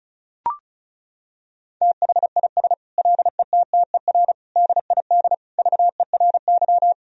Morse code, heard as a keyed tone, about 36 wpm, 700 Hz. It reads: THIS LETTER DID VERY
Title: Sentences from Top 1000 Words - Review Part 4 - 35wpm